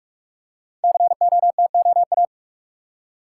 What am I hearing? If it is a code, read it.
Morse code, heard as a keyed tone, about 44 wpm, 700 Hz. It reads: COTOA
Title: Code Groups - 5 Letters - 45wpm